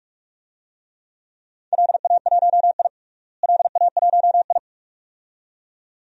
Morse code, heard as a keyed tone, about 41 wpm, 700 Hz. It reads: LA1I LA1I
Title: DXCC-100 Call Signs - 4 Characters - Repeated 2x - 45wpm